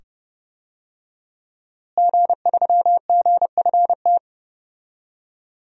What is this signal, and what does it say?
Morse code, heard as a keyed tone, about 29 wpm, 700 Hz. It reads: G3GFT
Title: DXCC-100 Call Signs - 5 Characters - Repeated 2x - 30wpm